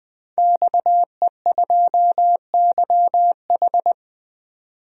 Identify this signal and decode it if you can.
Morse code, heard as a keyed tone, about 20 wpm, 700 Hz. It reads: XE2YH
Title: DXCC-100 Call Signs - 5 Characters - 20wpm